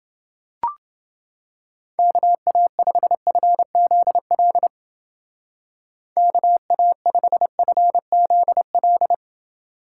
Morse code, heard as a keyed tone, about 30 wpm, 700 Hz. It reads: KA5FZL KA5FZL